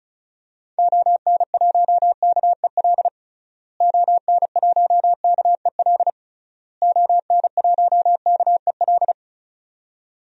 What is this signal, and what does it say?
Morse code, heard as a keyed tone, about 34 wpm, 700 Hz. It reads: ON1KEL ON1KEL ON1KEL
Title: DXCC-100 Call Signs - 6 Characters - Repeated 3x - 35wpm